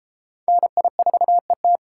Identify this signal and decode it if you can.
Morse code, heard as a keyed tone, about 33 wpm, 700 Hz. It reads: NI4ET